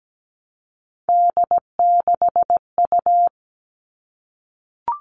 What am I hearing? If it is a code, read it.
Morse code, heard as a keyed tone, about 17 wpm, 700 Hz. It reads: D6U
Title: All Cut Numbers - 3-Digit 17wpm